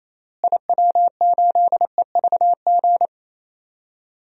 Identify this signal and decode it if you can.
Morse code, heard as a keyed tone, about 28 wpm, 700 Hz. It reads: IW8EVG
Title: DXCC-100 Call Signs - 6 Characters - Repeated 2x - 28wpm